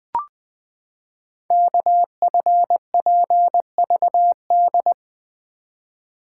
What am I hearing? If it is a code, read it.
Morse code, heard as a keyed tone, about 20 wpm, 700 Hz. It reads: KFPVD
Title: Code Groups - 5 Letters - 20wpm